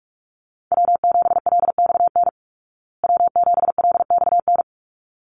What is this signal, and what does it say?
Morse code, heard as a keyed tone, about 44 wpm, 700 Hz. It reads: W7LXN W7LXN
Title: Call Signs Repeated 3x - 45wpm